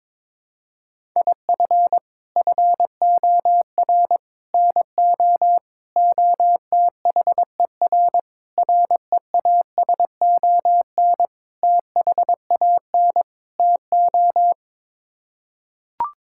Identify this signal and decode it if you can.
Morse code, heard as a keyed tone, about 22 wpm, 700 Hz. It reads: IF FOR NO OTHER REASON THAN TO
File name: practice / 22wpm / Sets of 7 Words - 22wpm